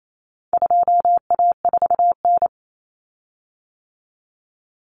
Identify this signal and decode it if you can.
Morse code, heard as a keyed tone, about 28 wpm, 700 Hz. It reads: 2A4N
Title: Common Cut Numbers - 4-Digit 28wpm